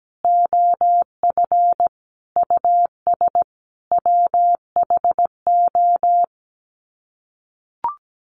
Morse code, heard as a keyed tone, about 17 wpm, 700 Hz. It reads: OF US WHO
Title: Sets of 5 Words Encore 17wpm